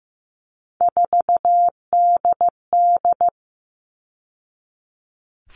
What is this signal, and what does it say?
Morse code, heard as a keyed tone, about 15 wpm, 700 Hz. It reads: 4DD
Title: All Cut Numbers - 3-Digit 15wpm